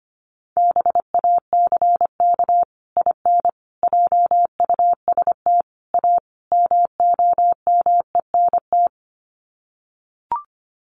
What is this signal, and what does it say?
Morse code, heard as a keyed tone, about 25 wpm, 690 Hz. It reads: BACK IN JUST A MOMENT
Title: Sets of 5 Words Encore 25wpm